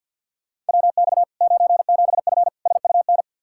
Morse code, heard as a keyed tone, about 52 wpm, 700 Hz. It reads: WX 97F SUN
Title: Practice QSO - Exchange 2 - Info - 50wpm